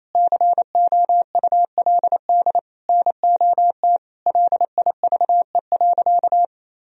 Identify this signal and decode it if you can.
Morse code, heard as a keyed tone, about 27 wpm, 700 Hz. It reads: COULD NOT LIVE.
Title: Sentences from Top 1000 Words - Review Part 4 - 28wpm